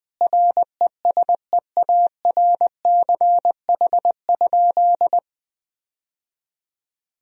Morse code, heard as a keyed tone, about 20 wpm, 700 Hz. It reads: RESEARCH?
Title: Sentences from Top 1000 Words - Review Part 4 - 20wpm